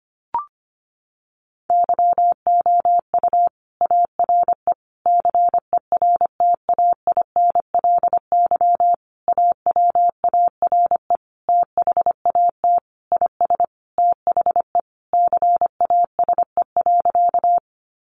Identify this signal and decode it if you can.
Morse code, heard as a keyed tone, about 25 wpm, 700 Hz. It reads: YOU ARE CERTAINLY AWARE THAT IS THE CASE.
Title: Sentences from Top 1200 Words 25wpm